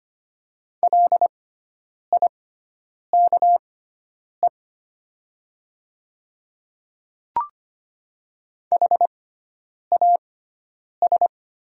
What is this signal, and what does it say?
Morse code, heard as a keyed tone, about 25 wpm, spaced as for 9 wpm, 700 Hz. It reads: LIKE HAS
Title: 6x Character Spacing - Top 100 Words - 25wpm